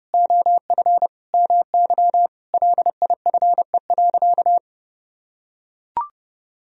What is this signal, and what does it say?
Morse code, heard as a keyed tone, about 30 wpm, 700 Hz. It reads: OF MY LIFE.
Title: Sentences from Top 1000 Words - Review Part 2 - 30wpm